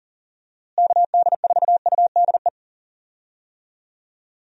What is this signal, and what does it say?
Morse code, heard as a keyed tone, about 40 wpm, 700 Hz. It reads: KD4UDE
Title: USA Call Signs - 6 Characters - Repeated 3x - 40wpm